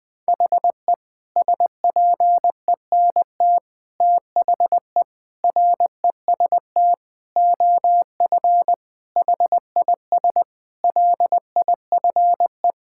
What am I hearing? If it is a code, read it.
Morse code, heard as a keyed tone, about 20 wpm, 700 Hz. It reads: HE SPENT THE REST OF HIS LIFE
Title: Sentences with 5-Word Phrases - Spoken Beginning - 20wpm